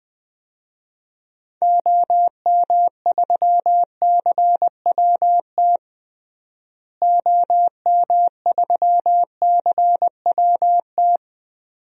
Morse code, heard as a keyed tone, about 21 wpm, 700 Hz. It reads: OM3CWT OM3CWT
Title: DXCC-100 Call Signs - 6 Characters - Repeated 2x - 20wpm